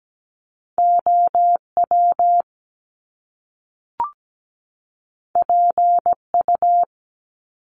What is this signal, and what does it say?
Morse code, heard as a keyed tone, about 17 wpm, 700 Hz. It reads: OW PU
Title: Top Two Letter Combinations in Words 17wpm